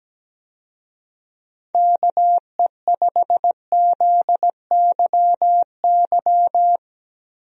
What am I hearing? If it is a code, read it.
Morse code, heard as a keyed tone, about 17 wpm, 700 Hz. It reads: KE5ZYY
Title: USA Call Signs - 6 Characters - Repeated 2x - 17wpm